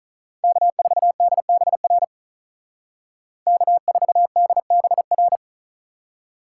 Morse code, heard as a keyed tone, about 43 wpm, 700 Hz. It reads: K4DBR K4DBR
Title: Speed Racing - USA 1x3 Call Signs 35wpm